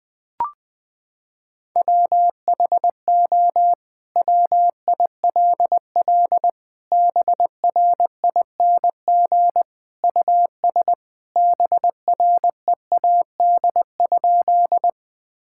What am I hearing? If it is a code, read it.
Morse code, heard as a keyed tone, about 20 wpm, 700 Hz. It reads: WHO WILL BRING US BREAD?
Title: Sentences from Top 1600 Words 20wpm